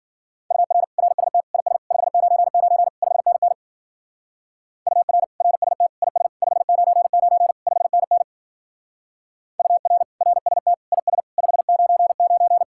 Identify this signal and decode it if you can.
Morse code, heard as a keyed tone, about 51 wpm, 700 Hz. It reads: UR RST IS 599 5NN UR RST IS 599 5NN UR RST IS 599